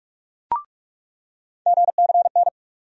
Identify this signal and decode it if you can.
Morse code, heard as a keyed tone, about 44 wpm, 700 Hz. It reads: GCN
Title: USA Call Sign Postfixes - 3 Letters 45wpm